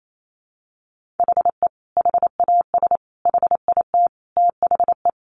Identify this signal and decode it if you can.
Morse code, heard as a keyed tone, about 28 wpm, 700 Hz. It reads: HE HAS HIT THE
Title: Sentences from Top 1200 Words 28wpm